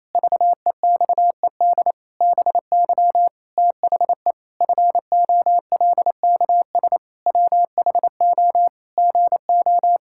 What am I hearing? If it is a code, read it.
Morse code, heard as a keyed tone, about 28 wpm, 700 Hz. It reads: VEXED BY THE FOLKS WHO GO
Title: Pangram Challenge v2 - 28wpm